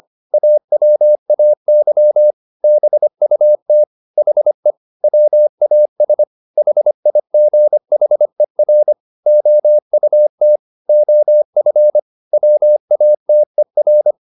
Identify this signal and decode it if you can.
Morse code, heard as a keyed tone, about 25 wpm, 600 Hz. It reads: AWAY BUT HE WAS HIGHER OUT OF WATER